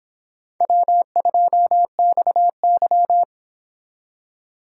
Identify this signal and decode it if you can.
Morse code, heard as a keyed tone, about 26 wpm, 700 Hz. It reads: W2XY